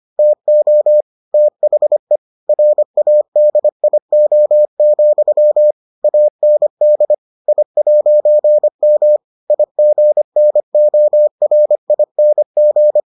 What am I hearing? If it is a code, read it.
Morse code, heard as a keyed tone, about 25 wpm, 600 Hz. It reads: TO THE RADIO, AND I'M IGNORING